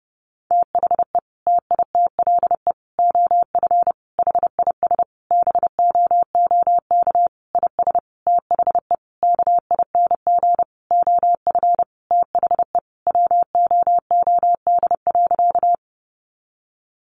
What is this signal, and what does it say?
Morse code, heard as a keyed tone, about 30 wpm, 700 Hz. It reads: THE TITLE OF HIS BOOK IS THE KING OF THE WOOD.
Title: Sentences from Top 1200 Words 30wpm